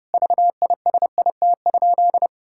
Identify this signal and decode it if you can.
Morse code, heard as a keyed tone, about 32 wpm, 700 Hz. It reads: VISIT?
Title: Sentences from Top 1600 Words 30wpm